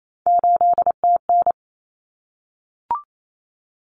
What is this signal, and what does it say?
Morse code, heard as a keyed tone, about 28 wpm, 700 Hz. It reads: 8TN
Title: Common Cut Numbers - 3-Digit 28wpm